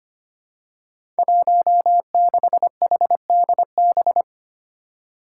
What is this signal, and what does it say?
Morse code, heard as a keyed tone, about 25 wpm, 700 Hz. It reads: 16HDB
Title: Five Challenging Letter-Number Combinations 25wpm